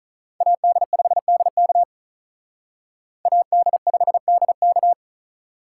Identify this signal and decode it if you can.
Morse code, heard as a keyed tone, about 41 wpm, 700 Hz. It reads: AD5DK AD5DK
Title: Speed Racing - USA 2x2 Call Signs 35wpm